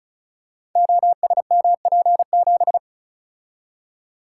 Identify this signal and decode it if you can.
Morse code, heard as a keyed tone, about 36 wpm, 690 Hz. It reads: OSMP7
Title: Code Groups - 5 Letter:Numbers - 35wpm